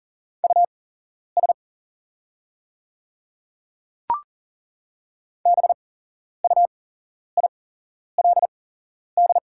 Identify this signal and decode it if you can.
Morse code, heard as a keyed tone, about 40 wpm, spaced as for 11 wpm, 700 Hz. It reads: US BUILD
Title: 8x Character Spacing - Top 200 Words - 40wpm